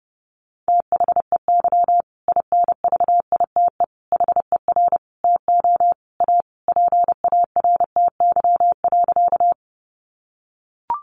Morse code, heard as a keyed tone, about 30 wpm, 700 Hz. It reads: THEY INVITE HER TO A PARTY.
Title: Easy Phrases 30wpm